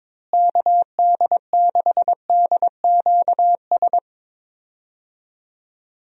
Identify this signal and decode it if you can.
Morse code, heard as a keyed tone, about 22 wpm, 700 Hz. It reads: KD6DQS
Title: USA Call Signs - 6 Characters - Repeated 2x - 22wpm